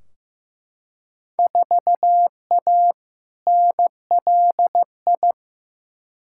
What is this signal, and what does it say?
Morse code, heard as a keyed tone, about 16 wpm, 700 Hz. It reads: 4A NLI
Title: Field Day Exchanges Encore - 15wpm